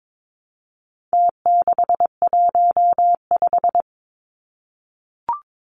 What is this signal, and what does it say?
Morse code, heard as a keyed tone, about 22 wpm, 700 Hz. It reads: T615
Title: Numbers - Contest Style - 4-Digit 22wpm